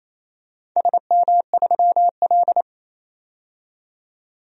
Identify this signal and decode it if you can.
Morse code, heard as a keyed tone, about 27 wpm, 700 Hz. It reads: SM3L
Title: DXCC-100 Call Signs - 4 Characters - 28wpm